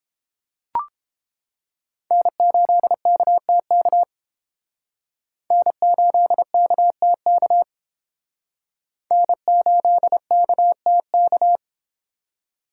Morse code, heard as a keyed tone, about 30 wpm, 700 Hz. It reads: N8KTK N8KTK N8KTK